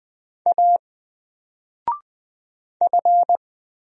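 Morse code, heard as a keyed tone, about 20 wpm, 700 Hz. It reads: A F
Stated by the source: Single letter and number Morse drill at 20 wpm, from N4LCD.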